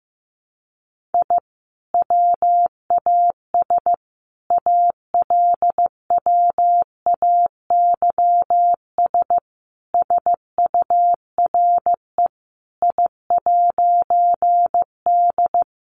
Morse code, heard as a keyed tone, about 15 wpm, 700 Hz. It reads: I WAS ALWAYS SURE I'D
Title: A Sound Of Thunder 15wpm